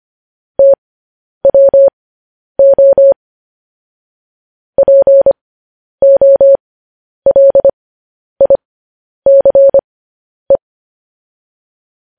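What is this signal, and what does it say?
Morse code, heard as a keyed tone, about 25 wpm, spaced as for 10 wpm, 550 Hz. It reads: TWO POLICE
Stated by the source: Morse code news headlines sent at 10 wpm for practice.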